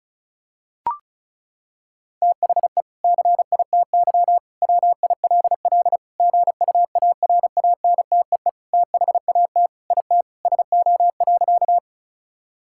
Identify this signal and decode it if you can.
Morse code, heard as a keyed tone, about 34 wpm, 700 Hz. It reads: THE CITY WILL GUARANTEE THAT IT SO.
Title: Sentences from Top 1600 Words 35wpm